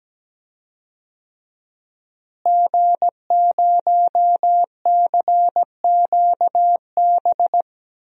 Morse code, heard as a keyed tone, about 17 wpm, 700 Hz. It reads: G0CQB
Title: DXCC-100 Call Signs - 5 Characters - Repeated 2x - 17wpm